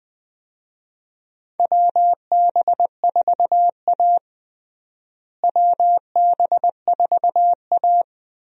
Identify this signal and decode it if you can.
Morse code, heard as a keyed tone, about 20 wpm, 700 Hz. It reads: WB4A WB4A
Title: USA Call Signs - 4 Characters - Repeated 3x - 20wpm